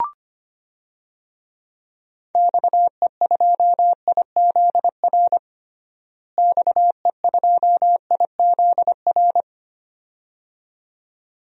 Morse code, heard as a keyed tone, about 25 wpm, 700 Hz. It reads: XE2IZR XE2IZR
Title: DXCC-100 Call Signs - 6 Characters - Repeated 2x - 25wpm